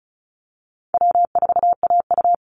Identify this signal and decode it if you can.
Morse code, heard as a keyed tone, about 34 wpm, 700 Hz. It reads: W4AU
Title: Easy Call Signs Repeated 2x - 35wpm